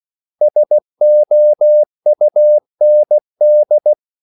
Morse code, heard as a keyed tone, about 16 wpm, 600 Hz. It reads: SOUND
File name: CW / Practice / 500MostCommonEnglishWords16